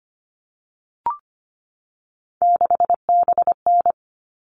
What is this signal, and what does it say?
Morse code, heard as a keyed tone, about 25 wpm, 700 Hz. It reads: 6BN
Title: All Cut Numbers - 3-Digit 25wpm